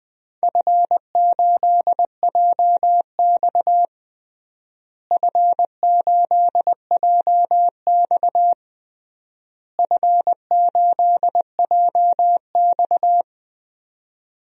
Morse code, heard as a keyed tone, about 20 wpm, 700 Hz. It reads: F8JX F8JX F8JX
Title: DXCC-100 Call Signs - 4 Characters - Repeated 3x - 20wpm